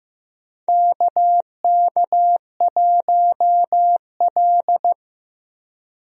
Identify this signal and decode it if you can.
Morse code, heard as a keyed tone, about 15 wpm, 700 Hz. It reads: KK1L
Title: USA Call Signs - 4 Characters - Repeated 2x - 15wpm